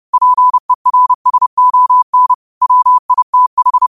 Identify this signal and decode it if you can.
Morse code, heard as a keyed tone, about 30 wpm, 1000 Hz. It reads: PERSON WITH
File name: qotdfiles / 30wpm